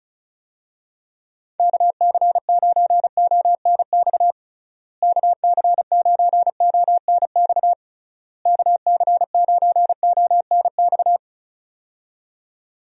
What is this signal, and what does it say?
Morse code, heard as a keyed tone, about 36 wpm, 700 Hz. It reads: KC9ONX KC9ONX KC9ONX
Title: USA Call Signs - 6 Characters - Repeated 3x - 35wpm